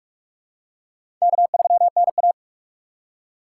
Morse code, heard as a keyed tone, about 45 wpm, 700 Hz. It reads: K3NA
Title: USA Call Signs - 4 Characters - 3x Character Spacing - 45wpm